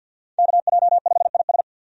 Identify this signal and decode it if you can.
Morse code, heard as a keyed tone, about 53 wpm, 700 Hz. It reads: KJ5IS
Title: USA Call Signs - 5 Characters - Repeated 2x - 50wpm